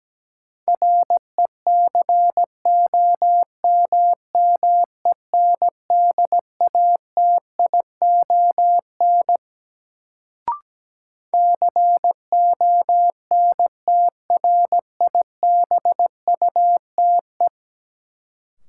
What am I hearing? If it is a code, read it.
Morse code, heard as a keyed tone, about 17 wpm, 700 Hz. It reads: RECOMMENDATION CONTRIBUTE
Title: Sentences from Top 1400 Words - New Word Drill - 17wpm